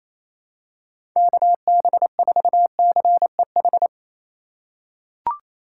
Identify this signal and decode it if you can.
Morse code, heard as a keyed tone, about 28 wpm, 700 Hz. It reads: KB4CEH
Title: USA Call Signs - 6 Characters - 2x Character Spacing - 28wpm